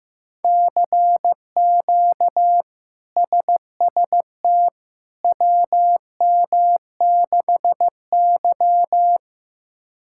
Morse code, heard as a keyed tone, about 15 wpm, 700 Hz. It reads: CQ SST WM6Y
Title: K1USN SST QSO Practice - 15wpm